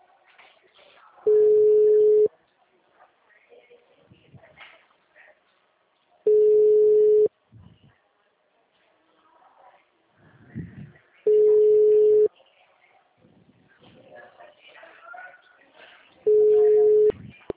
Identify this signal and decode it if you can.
Ringback tone (Europe, 425 Hz), with speech.